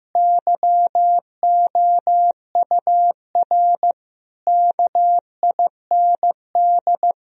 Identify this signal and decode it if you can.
Morse code, heard as a keyed tone, about 15 wpm, 700 Hz. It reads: YOUR KIND
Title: Sentences from Top 1000 Words - Review Part 4 - 15wpm